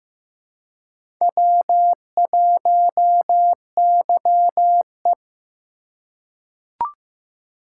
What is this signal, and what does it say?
Morse code, heard as a keyed tone, about 15 wpm, 700 Hz. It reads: W1YE
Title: USA Call Signs - 4 Characters - 4x Character Spacing - 15wpm